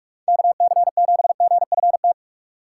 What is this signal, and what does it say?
Morse code, heard as a keyed tone, about 41 wpm, 700 Hz. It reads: KC7GFT